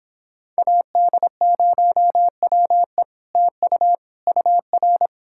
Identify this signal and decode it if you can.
Morse code, heard as a keyed tone, about 29 wpm, 700 Hz. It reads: AD0WE TU UR